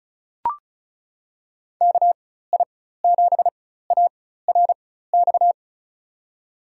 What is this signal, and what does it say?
Morse code, heard as a keyed tone, about 36 wpm, spaced as for 16 wpm, 700 Hz. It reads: KI7ARX
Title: USA Call Signs - 6 Characters - 4x Character Spacing - 35wpm